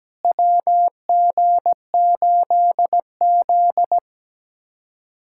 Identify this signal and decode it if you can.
Morse code, heard as a keyed tone, about 17 wpm, 700 Hz. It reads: WG8Z
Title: USA Callsign DE Callsign - 4 Characters - 17wpm